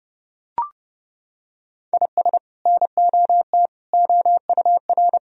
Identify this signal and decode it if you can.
Morse code, heard as a keyed tone, about 30 wpm, 700 Hz. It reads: IS NOT OUR
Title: Sentences from Top 1000 Words - Review Part 3 - 30wpm